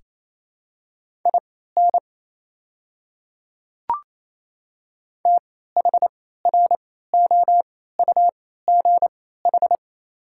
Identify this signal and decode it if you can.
Morse code, heard as a keyed tone, about 28 wpm, spaced as for 16 wpm, 700 Hz. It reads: IN THROUGH
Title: 3x Character Spacing - Top 500 Words - 28wpm